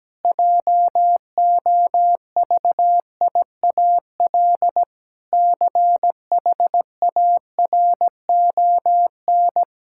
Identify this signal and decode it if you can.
Morse code, heard as a keyed tone, about 17 wpm, 700 Hz. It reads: JOVIAL CHARON
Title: Pangram Challenge v2 - 17wpm